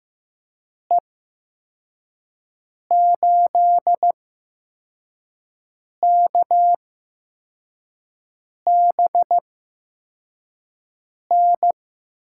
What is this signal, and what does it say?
Morse code, heard as a keyed tone, about 15 wpm, spaced as for 4 wpm, 700 Hz. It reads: E8KBN
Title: USA Call Signs - 6 Characters - 8x Character Spacing - 15wpm